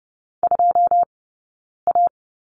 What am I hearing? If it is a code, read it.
Morse code, heard as a keyed tone, about 30 wpm, spaced as for 9 wpm, 690 Hz. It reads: 2A
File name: course / 30:10wpm / 086 Lesson 15d - Number 5 - Two Characters 30wpm